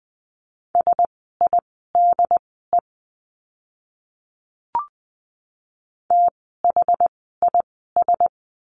Morse code, heard as a keyed tone, about 20 wpm, spaced as for 14 wpm, 700 Hz. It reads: SIDE THIS
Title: 108 Lesson 18f - Letter m - Words 20wpm